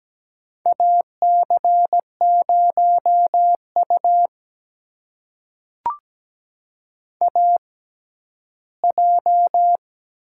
Morse code, heard as a keyed tone, about 17 wpm, spaced as for 11 wpm, 700 Hz. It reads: AC0U AJ
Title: USA Call Signs - 4 Characters - 6x Character Spacing - 17wpm